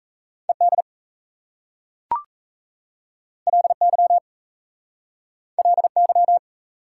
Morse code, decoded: ED LY LY

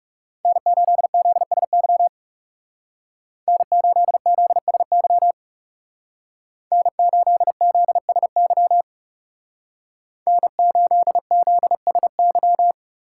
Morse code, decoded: N8ZSY N8ZSY N8ZSY N8ZSY

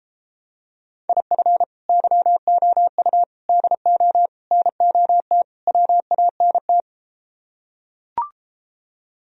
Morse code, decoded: IF YOU DO NOT WANT

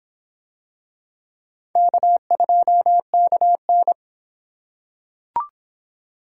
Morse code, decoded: K2KN

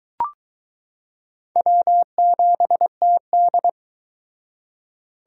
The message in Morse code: W7TD

